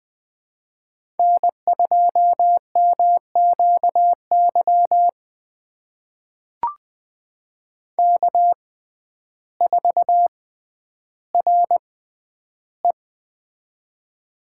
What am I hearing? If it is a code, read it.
Morse code, heard as a keyed tone, about 20 wpm, spaced as for 11 wpm, 700 Hz. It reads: N2MQY K4RE
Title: USA Call Signs - 6x Character Spacing - 20wpm